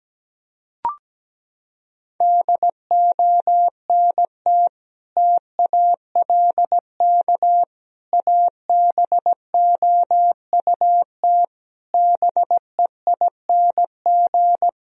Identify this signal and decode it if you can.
Morse code, heard as a keyed tone, about 17 wpm, 700 Hz. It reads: DONT TALK ABOUT BEING
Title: Sentences from Top 1000 Words - Review Part 4 - 17wpm